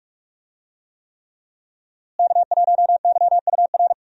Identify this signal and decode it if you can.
Morse code, heard as a keyed tone, about 46 wpm, 700 Hz. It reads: K1YUR
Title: USA Call Signs - 5 Characters - Repeated 3x - 45wpm